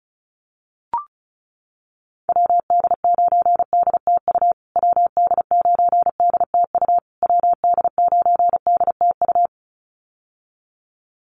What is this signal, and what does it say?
Morse code, heard as a keyed tone, about 34 wpm, 700 Hz. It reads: WD9DTU WD9DTU WD9DTU
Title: USA Call Signs Repeated 3x 35wpm